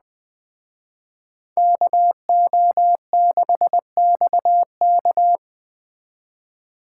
Morse code, heard as a keyed tone, about 20 wpm, 700 Hz. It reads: KO6XK
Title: Speed Racing - USA 2x2 Call Signs 17wpm